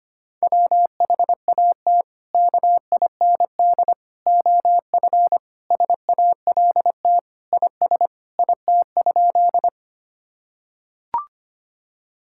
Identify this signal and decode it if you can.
Morse code, heard as a keyed tone, about 25 wpm, 700 Hz. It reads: WHAT KIND OF SALT IS IT?
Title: Sentences from Top 1000 Words - Review Part 5 - 25wpm